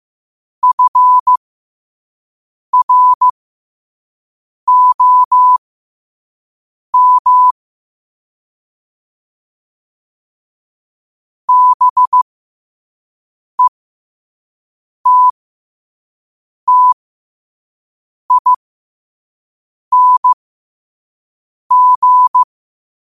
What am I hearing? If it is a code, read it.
Morse code, heard as a keyed tone, about 15 wpm, spaced as for 5 wpm, 1000 Hz. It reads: FROM BETTING
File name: qotdfiles / 5wpm